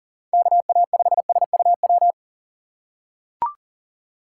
Morse code, heard as a keyed tone, about 40 wpm, 700 Hz. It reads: KA5SUW
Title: USA Call Signs - 6 Characters - 4x Character Spacing - 40wpm